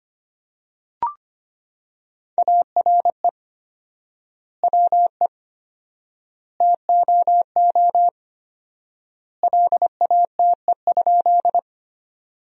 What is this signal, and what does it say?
Morse code, heard as a keyed tone, about 25 wpm, 700 Hz. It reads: ARE WE TOO LATE?
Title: 4x Word-Spacing - Sentences from Top 300 Words - 25wpm